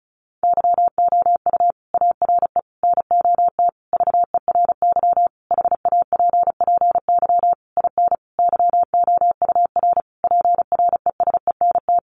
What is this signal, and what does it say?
Morse code, heard as a keyed tone, about 34 wpm, 700 Hz. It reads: YOU ARE NOT VERY HAPPY IN YOUR PRESENT
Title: Sentences from Top 1200 Words 35wpm